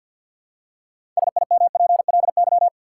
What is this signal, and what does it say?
Morse code, heard as a keyed tone, about 53 wpm, 700 Hz. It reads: SIMPLY
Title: Six Letter Words - 50wpm